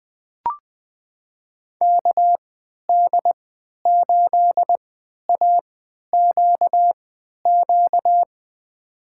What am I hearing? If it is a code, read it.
Morse code, heard as a keyed tone, about 21 wpm, spaced as for 11 wpm, 700 Hz. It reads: KD8AQQ
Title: USA Call Signs - 6 Characters - 3x Character Spacing - 20wpm